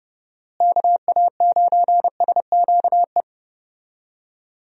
Morse code, heard as a keyed tone, about 30 wpm, 700 Hz. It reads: KA9SQE
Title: USA Call Signs - 6 Characters - Repeated 2x - 30wpm